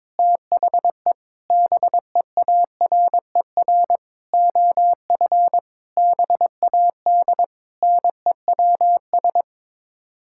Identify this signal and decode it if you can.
Morse code, heard as a keyed tone, about 22 wpm, 700 Hz. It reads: THE BEARER OF BAD NEWS
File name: Sets of 9 Words - 22wpm